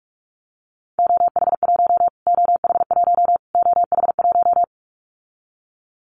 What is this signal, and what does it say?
Morse code, heard as a keyed tone, about 44 wpm, 700 Hz. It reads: OH1 OH1 OH1
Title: Callsign Prefixes for Top 25 DXCC - Repeated 3x - 45wpm